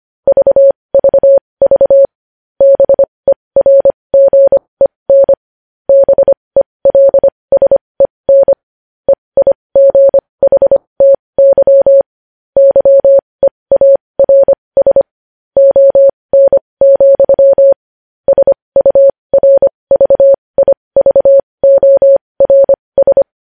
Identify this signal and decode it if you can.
Morse code, heard as a keyed tone, about 25 wpm, spaced as for 20 wpm, 550 Hz. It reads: VVV BERGEN BELSEN EIGHTY YEARS ON, SURVIVORS
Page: news headlines in morse code at 20 wpm